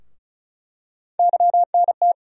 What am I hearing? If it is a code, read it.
Morse code, heard as a keyed tone, about 36 wpm, 700 Hz. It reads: YNT
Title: USA Call Sign Postfixes - 3 Letters 35wpm